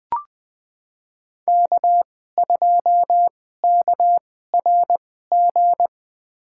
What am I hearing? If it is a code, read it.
Morse code, heard as a keyed tone, about 20 wpm, spaced as for 14 wpm, 700 Hz. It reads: K2KRG
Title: USA Call Signs - 2x Character Spacing - 20wpm